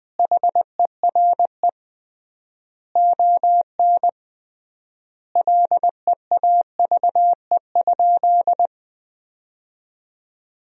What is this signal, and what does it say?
Morse code, heard as a keyed tone, about 20 wpm, 700 Hz. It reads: HERE ON LEAVE?
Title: 3x Word-Spacing - Sentences from Top 400 Words - 20wpm